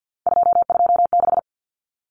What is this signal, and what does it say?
Morse code, heard as a keyed tone, about 50 wpm, 700 Hz. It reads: 226
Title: Numbers - 3-Digit 50wpm